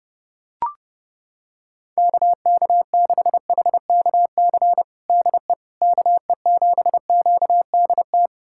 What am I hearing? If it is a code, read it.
Morse code, heard as a keyed tone, about 31 wpm, 700 Hz. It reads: KK6HKC DE KE7QDT
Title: USA Callsign DE Callsign - 6 Characters - 30wpm